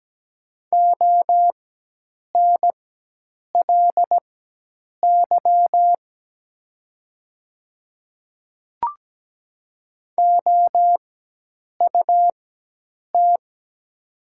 Morse code, heard as a keyed tone, about 17 wpm, spaced as for 8 wpm, 690 Hz. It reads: ONLY OUT
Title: 4x Character Spacing - Top 200 Words - 17wpm